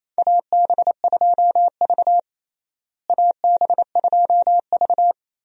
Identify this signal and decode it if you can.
Morse code, heard as a keyed tone, about 27 wpm, 700 Hz. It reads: AB2V AB2V
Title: USA Call Signs - 4 Characters - Repeated 2x - 28wpm